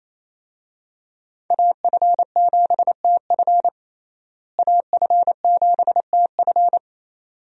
Morse code, heard as a keyed tone, about 28 wpm, 700 Hz. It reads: AF7TF AF7TF
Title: USA Call Signs - 5 Characters - Repeated 2x - 28wpm